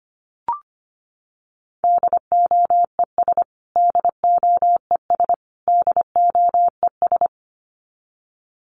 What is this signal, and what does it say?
Morse code, heard as a keyed tone, about 25 wpm, 700 Hz. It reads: DOES DOES DOES
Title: Top 500 Words Repeated 25wpm